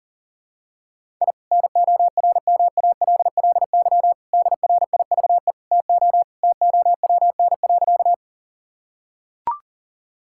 Morse code, heard as a keyed tone, about 37 wpm, 700 Hz. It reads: I NORMALLY DRIVE TO TOWN.
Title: Sentences from Top 1000 Words - Review Part 6 - 40wpm